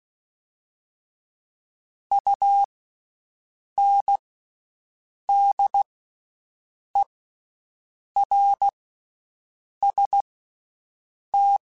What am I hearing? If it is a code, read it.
Morse code, heard as a keyed tone, about 16 wpm, spaced as for 6 wpm, 800 Hz. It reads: UNDERST
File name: Special INGs-mp3s / Special INGs-mp3s / Slow ing @15